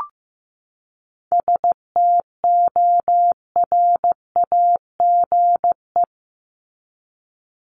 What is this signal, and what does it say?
Morse code, heard as a keyed tone, about 15 wpm, 700 Hz. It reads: STORAGE
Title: Top 1251-1500 Words 15wpm